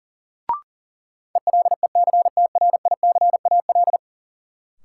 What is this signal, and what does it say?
Morse code, heard as a keyed tone, about 40 wpm, 700 Hz. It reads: ELECTRICAL
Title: Sentences from Top 1000 Words - Review Part 6 - New Word Drill - 40wpm